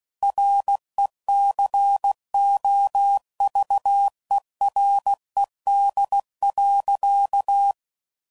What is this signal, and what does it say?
Morse code, heard as a keyed tone, about 16 wpm, 800 Hz. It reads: RECOVERED.